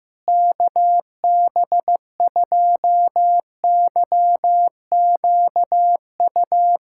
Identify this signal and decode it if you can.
Morse code, heard as a keyed tone, about 15 wpm, 700 Hz. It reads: KB2YQU
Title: USA Callsign DE Callsign 15wpm